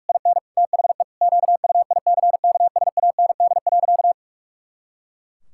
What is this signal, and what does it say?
Morse code, heard as a keyed tone, about 46 wpm, 700 Hz. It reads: IN THE QUICKSAND.
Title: Pangram Challenge v2 - 45wpm